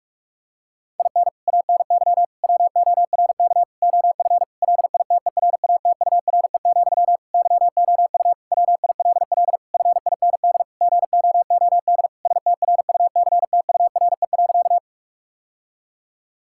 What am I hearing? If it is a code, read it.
Morse code, heard as a keyed tone, about 41 wpm, 690 Hz. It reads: IN ANY WORK OF LITERATURE, YOU WILL FIND GOOD STRUCTURE.
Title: Sentences from Top 1000 Words - Review Part 6 - 45wpm